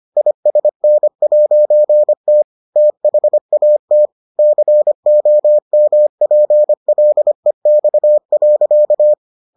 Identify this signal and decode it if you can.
Morse code, heard as a keyed tone, about 25 wpm, 600 Hz. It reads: ISN'T THAT COMPLEX.